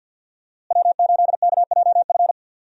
Morse code, heard as a keyed tone, about 45 wpm, 700 Hz. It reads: W8KJF
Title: USA Call Signs - 5 Characters - Repeated 3x - 50wpm